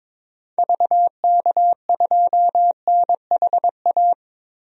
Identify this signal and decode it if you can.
Morse code, heard as a keyed tone, about 22 wpm, 700 Hz. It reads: VK2NHA
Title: DXCC-100 Call Signs - 6 Characters - 22wpm